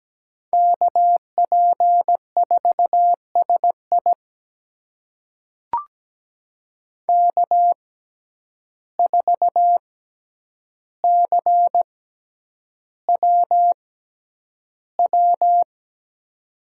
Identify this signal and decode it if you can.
Morse code, heard as a keyed tone, about 17 wpm, spaced as for 9 wpm, 700 Hz. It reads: KP4SI K4CWW
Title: USA Call Signs - 5 Characters - 6x Character Spacing - 17wpm